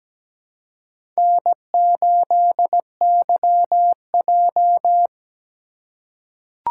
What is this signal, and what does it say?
Morse code, heard as a keyed tone, about 17 wpm, 700 Hz. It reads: N8YJ
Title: USA Call Signs - 3x Character Spacing - 17wpm